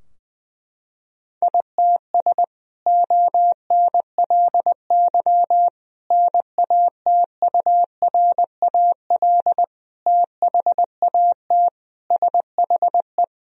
Morse code, heard as a keyed tone, about 19 wpm, 700 Hz. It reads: ITS ONLY NATURAL THAT SHE
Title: Sentences from Top 1000 Words - Review Part 4 - 20wpm